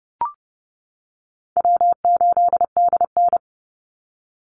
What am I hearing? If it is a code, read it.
Morse code, heard as a keyed tone, about 30 wpm, 700 Hz. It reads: W8DN
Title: Call Signs - 30wpm